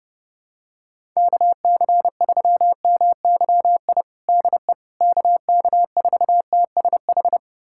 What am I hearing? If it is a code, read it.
Morse code, heard as a keyed tone, about 30 wpm, 700 Hz. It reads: KC3MYI DE KK4TSH
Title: USA Callsign DE Callsign - 6 Characters - 30wpm